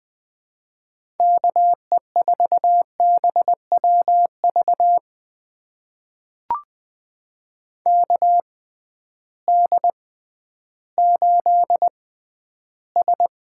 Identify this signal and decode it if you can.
Morse code, heard as a keyed tone, about 20 wpm, spaced as for 12 wpm, 700 Hz. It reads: KE4BWV KD8S
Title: USA Call Signs - 6 Characters - 6x Character Spacing - 20wpm